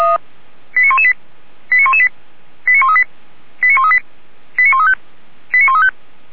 Five-tone selective call (ZVEI), heard as quick runs of five tones, tones of about 70 ms. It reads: 89118 89118 89138 89138 89135 89135